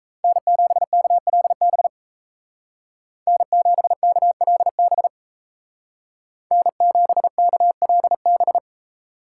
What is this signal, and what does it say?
Morse code, heard as a keyed tone, about 36 wpm, 700 Hz. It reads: N7KLB N7KLB N7KLB